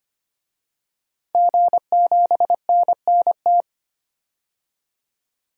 Morse code, heard as a keyed tone, about 25 wpm, 700 Hz. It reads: G7NNT
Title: DXCC-100 Call Signs - 5 Characters - 25wpm